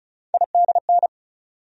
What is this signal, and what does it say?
Morse code, heard as a keyed tone, about 36 wpm, 700 Hz. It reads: IDN
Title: USA Call Sign Postfixes - 3 Letters 35wpm